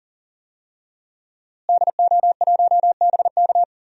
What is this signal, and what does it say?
Morse code, heard as a keyed tone, about 40 wpm, 700 Hz. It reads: DO1BK
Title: DXCC-100 Call Signs - 5 Characters - Repeated 3x - 40wpm